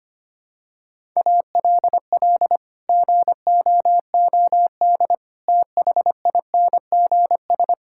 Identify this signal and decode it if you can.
Morse code, heard as a keyed tone, about 25 wpm, 700 Hz. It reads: ALL GOOD THINGS
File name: practice / 25wpm / Sentences with 9-Word Phrases - 25wpm